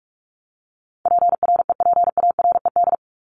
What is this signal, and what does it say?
Morse code, heard as a keyed tone, about 44 wpm, 700 Hz. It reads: PREPARED
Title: ED Suffix - Top 100 Words with Base Words 45wpm